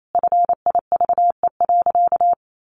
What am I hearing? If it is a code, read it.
Morse code, heard as a keyed tone, about 28 wpm, 700 Hz. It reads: FIVE.
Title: Sentences from Top 400 Words 28wpm